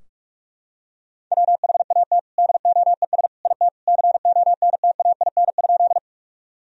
Morse code, heard as a keyed tone, about 46 wpm, 700 Hz. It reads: WHAT DOES IT CONTAIN?
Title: Sentences from Top 1000 Words - Review Part 2 - 45wpm